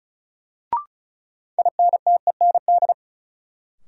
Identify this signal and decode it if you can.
Morse code, heard as a keyed tone, about 36 wpm, 700 Hz. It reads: INTEND
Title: Sentences from Top 1600 Words - New Word Drill - 35wpm